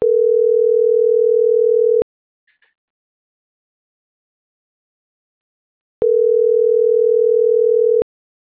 Ringback tone (North America), 440 + 480 Hz.